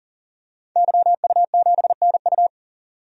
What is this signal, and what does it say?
Morse code, heard as a keyed tone, about 40 wpm, 700 Hz. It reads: YU7NU
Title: DXCC-100 Call Signs - 5 Characters - 40wpm